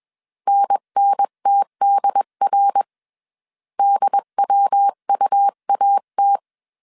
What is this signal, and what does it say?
Morse code, heard as a keyed tone, about 21 wpm, 800 Hz. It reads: NNTDR DWUAT
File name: CW / DET / 1 3 1500C S B 20wpm 1754